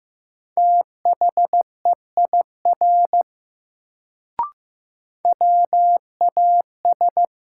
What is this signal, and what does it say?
Morse code, heard as a keyed tone, about 15 wpm, 700 Hz. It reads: THEIR WAS
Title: Sets of 2 Words - New Word Drill - 15wpm